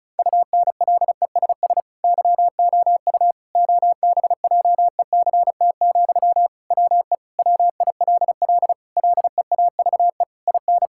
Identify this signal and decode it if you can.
Morse code, heard as a keyed tone, about 36 wpm, 700 Hz. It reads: UNLESS YOU OBJECT, WE WILL LEAVE IN
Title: Sentences from Top 1000 Words - Review Part 6 - 35wpm